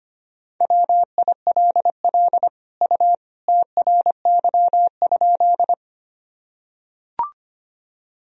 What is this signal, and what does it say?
Morse code, heard as a keyed tone, about 25 wpm, 700 Hz. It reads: WILL U TRY?
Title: Sentences from Top 200 Words and Abbreviations 25wpm